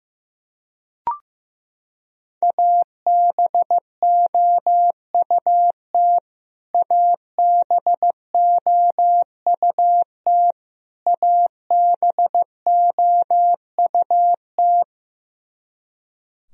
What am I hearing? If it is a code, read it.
Morse code, heard as a keyed tone, about 15 wpm, 700 Hz. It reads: ABOUT ABOUT ABOUT
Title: Top 100 Words Repeated 15wpm